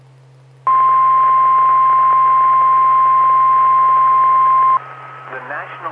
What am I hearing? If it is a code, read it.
Emergency Alert System (SAME) tones (the 1050 Hz attention tone), with speech. It is heard from 0.6 to 4.8 s.